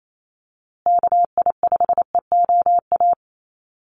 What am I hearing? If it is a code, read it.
Morse code, heard as a keyed tone, about 28 wpm, 700 Hz. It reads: KI5EOA
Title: USA Call Signs Repeated 3x 28wpm